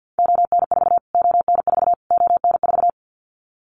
Morse code, heard as a keyed tone, about 50 wpm, 700 Hz. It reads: ON4 ON4 ON4
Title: Callsign Prefixes for Top 25 DXCC - Repeated 3x - 50wpm